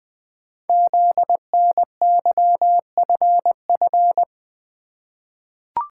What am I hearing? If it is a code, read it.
Morse code, heard as a keyed tone, about 20 wpm, 700 Hz. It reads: ZNYFF
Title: Code Groups - 5 Letters - 20wpm